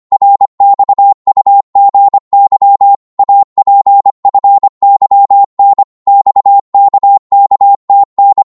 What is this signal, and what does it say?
Morse code, heard as a keyed tone, about 25 wpm, 800 Hz. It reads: RXUGY APFYN XKKTN
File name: Morse / ltr-25wpm